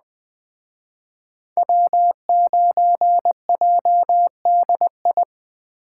Morse code, heard as a keyed tone, about 20 wpm, 700 Hz. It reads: W9JDI